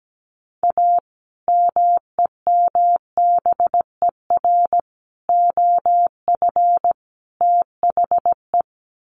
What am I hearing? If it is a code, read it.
Morse code, heard as a keyed tone, about 17 wpm, 700 Hz. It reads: A MEMBER OF THE
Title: Sets of 4 Words 17wpm